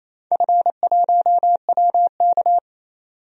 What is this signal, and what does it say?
Morse code, heard as a keyed tone, about 28 wpm, 700 Hz. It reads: F1WK
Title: DXCC-100 Call Signs - 4 Characters - 28wpm